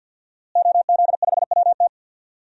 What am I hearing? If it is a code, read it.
Morse code, heard as a keyed tone, about 52 wpm, 700 Hz. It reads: OZ5WT
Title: DXCC-100 Call Signs - 5 Characters - Repeated 3x - 50wpm